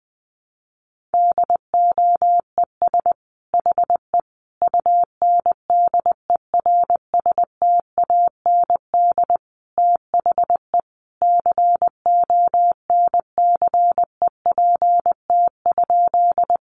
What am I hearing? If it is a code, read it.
Morse code, heard as a keyed tone, about 20 wpm, 700 Hz. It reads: DOES HE UNDERSTAND THE CONCEPT?
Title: Sentences from Top 1100 Words 20wpm